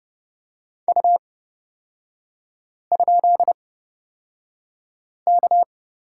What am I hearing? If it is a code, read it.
Morse code, heard as a keyed tone, about 30 wpm, spaced as for 5 wpm, 700 Hz. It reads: U?K
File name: Lesson 36e - BT - Three Characters 30wpm